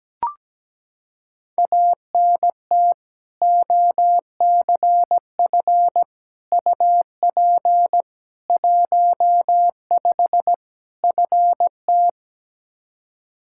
Morse code, heard as a keyed tone, about 17 wpm, 700 Hz. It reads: ANT OCF UP 15 FT
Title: Practice QSO - Exchange 2 - Info - 17wpm